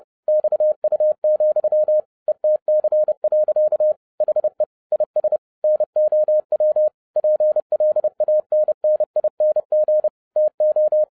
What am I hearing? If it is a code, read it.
Morse code, heard as a keyed tone, about 30 wpm, 610 Hz. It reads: XU, ETC. HE IS NOW PLANNING TO
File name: 30wpm